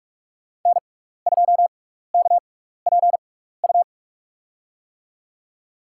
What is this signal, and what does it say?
Morse code, heard as a keyed tone, about 46 wpm, spaced as for 15 wpm, 700 Hz. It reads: N2KPU
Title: USA Call Signs - 6x Character Spacing - 45wpm